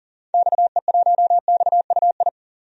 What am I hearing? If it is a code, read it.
Morse code, heard as a keyed tone, about 40 wpm, 700 Hz. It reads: XE1XUI
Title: DXCC-100 Call Signs - 6 Characters - 40wpm